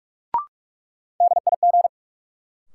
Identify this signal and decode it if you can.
Morse code, heard as a keyed tone, about 46 wpm, 700 Hz. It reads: DIG
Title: Sentences from Top 1600 Words - New Word Drill - 45wpm